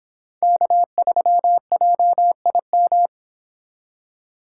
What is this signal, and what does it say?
Morse code, heard as a keyed tone, about 25 wpm, 700 Hz. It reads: K3JIM